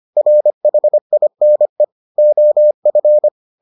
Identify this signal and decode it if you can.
Morse code, heard as a keyed tone, about 25 wpm, 600 Hz. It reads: RHINE OF